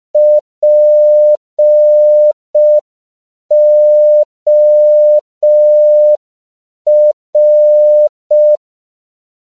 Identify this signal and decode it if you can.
Morse code, heard as a keyed tone, about 5 wpm, 600 Hz. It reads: POR
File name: CW / 5PPM